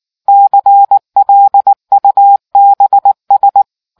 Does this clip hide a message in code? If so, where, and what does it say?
Yes: Morse code from 0.3 to 3.6 s, heard as a keyed tone, about 19 wpm, 800 Hz. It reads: CLUBS